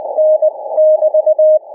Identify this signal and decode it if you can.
Morse code, heard as a keyed tone, about 20 wpm, 590 Hz. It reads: N <BT>
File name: cw / ssn 40M